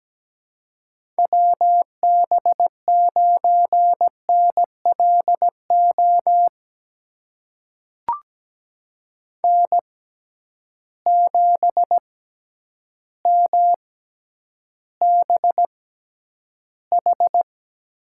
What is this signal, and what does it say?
Morse code, heard as a keyed tone, about 17 wpm, spaced as for 9 wpm, 700 Hz. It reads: WB9NLO N7MBH
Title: USA Call Signs - 6x Character Spacing - 17wpm